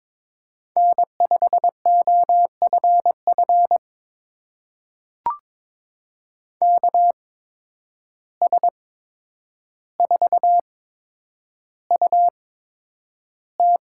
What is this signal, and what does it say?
Morse code, heard as a keyed tone, about 22 wpm, spaced as for 9 wpm, 700 Hz. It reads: N5OFF KS4UT
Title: USA Call Signs - 5 Characters - 8x Character Spacing - 22wpm